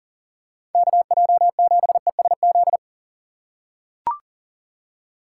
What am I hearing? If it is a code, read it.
Morse code, heard as a keyed tone, about 39 wpm, 700 Hz. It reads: KJ7ESZ
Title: USA Call Signs - 6 Characters - 3x Character Spacing - 40wpm